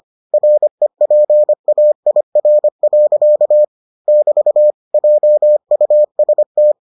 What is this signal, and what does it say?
Morse code, heard as a keyed tone, about 25 wpm, 600 Hz. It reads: REPAIR. <BT> JUST